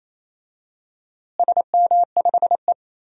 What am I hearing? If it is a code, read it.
Morse code, heard as a keyed tone, about 28 wpm, 700 Hz. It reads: SM5E
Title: DXCC-100 Call Signs - 4 Characters - Repeated 2x - 28wpm